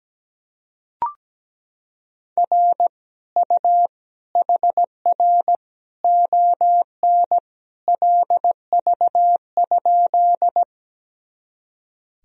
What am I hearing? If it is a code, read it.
Morse code, heard as a keyed tone, about 17 wpm, 700 Hz. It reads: R U HR ON LV?
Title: Sentences from Top 400 Words and Abbreviations 17wpm